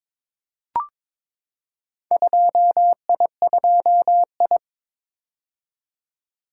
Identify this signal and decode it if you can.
Morse code, heard as a keyed tone, about 23 wpm, 700 Hz. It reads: 2I2I
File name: practice / 22wpm / Four Letter-Number Combinations - 22wpm